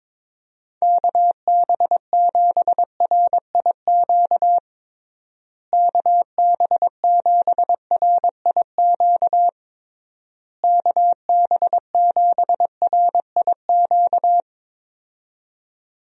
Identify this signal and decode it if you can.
Morse code, heard as a keyed tone, about 22 wpm, 700 Hz. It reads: KB7RIQ KB7RIQ KB7RIQ
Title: USA Call Signs - 6 Characters - Repeated 3x - 22wpm